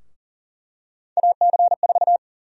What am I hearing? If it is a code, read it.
Morse code, heard as a keyed tone, about 40 wpm, 700 Hz. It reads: AC4
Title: USA Call Sign Prefixes - 40wpm